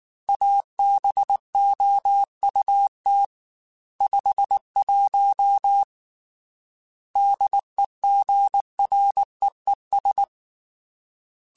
Morse code, heard as a keyed tone, about 19 wpm, 800 Hz. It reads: ABOUT 51 DEGREES
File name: Weather Sets / Weather Sets-mp3s / Weather 4 21 @18